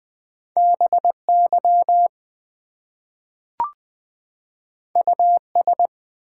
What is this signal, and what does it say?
Morse code, heard as a keyed tone, about 20 wpm, 700 Hz. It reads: BY US
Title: Two Letter Words - 20wpm